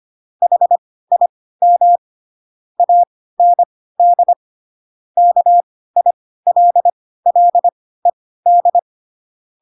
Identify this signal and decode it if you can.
Morse code, heard as a keyed tone, about 25 wpm, spaced as for 16 wpm, 700 Hz. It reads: HIM AND KILLED
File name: Pinocchio / Pinocchio ch7 10 25 ik1hge